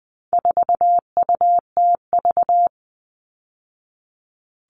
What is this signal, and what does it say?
Morse code, heard as a keyed tone, about 20 wpm, 700 Hz. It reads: 4UTV
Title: All Cut Numbers - 4-Digit 20wpm